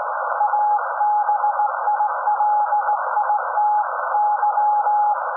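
Morse code, heard as a keyed tone, about 18 wpm, 840 Hz. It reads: ABILITY
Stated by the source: SAQ Grimeton Radio, Alexanderson 200 kW alternator on 17.2 kHz, received in Potsdam with a very strong signal on World Radio Day.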